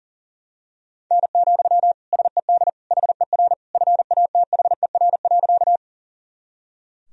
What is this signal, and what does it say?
Morse code, heard as a keyed tone, about 37 wpm, 700 Hz. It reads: N, SED HER FATHER.
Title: Sentences from Top 300 Words and Abbreviations 40wpm